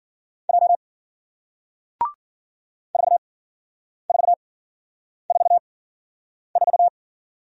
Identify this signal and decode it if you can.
Morse code, heard as a keyed tone, about 56 wpm, spaced as for 9 wpm, 700 Hz. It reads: J 4444